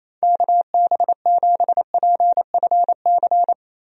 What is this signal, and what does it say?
Morse code, heard as a keyed tone, about 28 wpm, 700 Hz. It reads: KB7PFC
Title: USA Call Signs - 6 Characters - 28wpm